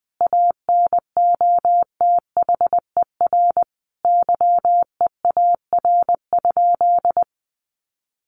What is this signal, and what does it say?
Morse code, heard as a keyed tone, about 20 wpm, 700 Hz. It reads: ANOTHER YEAR?
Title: Sentences from Top 1500 Words 20wpm